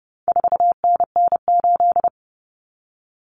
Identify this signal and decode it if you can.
Morse code, heard as a keyed tone, about 30 wpm, 700 Hz. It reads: 4NN8
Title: Common Cut Numbers - 4-Digit 30wpm